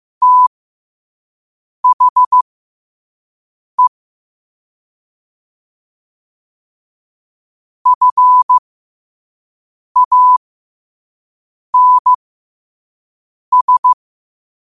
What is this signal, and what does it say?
Morse code, heard as a keyed tone, about 15 wpm, spaced as for 5 wpm, 1000 Hz. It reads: THE FANS
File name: qotdfiles / 5wpm